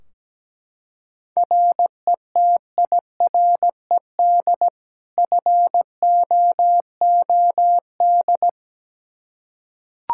Morse code, heard as a keyed tone, about 17 wpm, 700 Hz. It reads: RETIRED FOOD
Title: Practice QSO - Exchange 3 - More Info - 17wpm